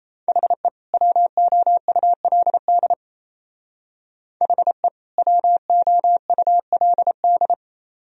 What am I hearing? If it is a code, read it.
Morse code, heard as a keyed tone, about 37 wpm, 690 Hz. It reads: HE WOULD HE WOULD